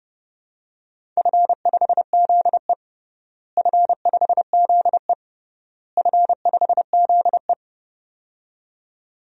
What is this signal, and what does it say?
Morse code, heard as a keyed tone, about 30 wpm, 700 Hz. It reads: F5ZE F5ZE F5ZE
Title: DXCC-100 Call Signs - 4 Characters - Repeated 3x - 30wpm